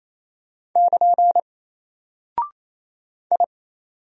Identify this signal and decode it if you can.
Morse code, heard as a keyed tone, about 28 wpm, 700 Hz. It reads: <KN> I
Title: Single Character (Letter:Number:Punctuation:Prosign) Alt 28wpm